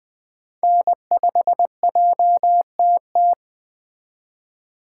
Morse code, heard as a keyed tone, about 20 wpm, 700 Hz. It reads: N5JTT
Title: Speed Racing - USA 1x3 Call Signs 17wpm